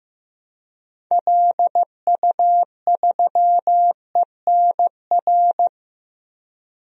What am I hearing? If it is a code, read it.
Morse code, heard as a keyed tone, about 15 wpm, 700 Hz. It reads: LU3ENR
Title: DXCC-100 Call Signs - 6 Characters - 15wpm